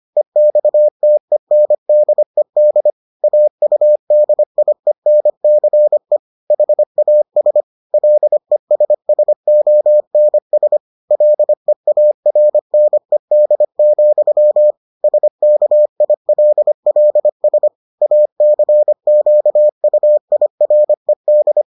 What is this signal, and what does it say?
Morse code, heard as a keyed tone, about 25 wpm, 600 Hz. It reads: EXTENDED AUDIENCE HAS LESSONS LEARNED, SKILLS ACQUIRED